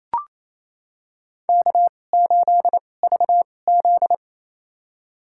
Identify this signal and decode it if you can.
Morse code, heard as a keyed tone, about 27 wpm, spaced as for 20 wpm, 700 Hz. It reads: K8VZ
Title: USA Call Signs - 4 Characters - 2x Character Spacing - 28wpm